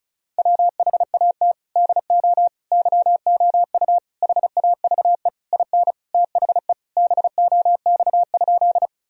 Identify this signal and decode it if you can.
Morse code, heard as a keyed tone, about 36 wpm, 700 Hz. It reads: WHAT DO YOU HAVE IN THE BOX?
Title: Sentences from Top 1000 Words - Review Part 2 - 35wpm